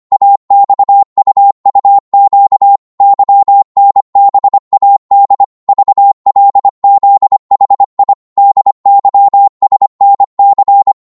Morse code, heard as a keyed tone, about 25 wpm, 800 Hz. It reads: AXUUQ YNBAD VLZHI DYSNC